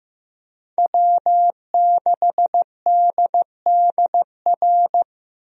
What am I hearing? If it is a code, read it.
Morse code, heard as a keyed tone, about 16 wpm, 700 Hz. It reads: W6DDR
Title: USA Call Signs - 5 Characters - Repeated 3x - 15wpm